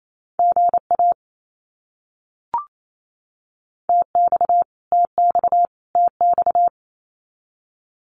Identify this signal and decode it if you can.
Morse code, heard as a keyed tone, about 28 wpm, 700 Hz. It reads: GA TX TX TX
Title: Abbreviations Repeated 3x - 28wpm